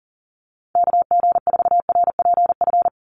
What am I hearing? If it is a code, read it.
Morse code, heard as a keyed tone, about 40 wpm, 700 Hz. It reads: KG4RPF
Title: USA Call Signs 40wpm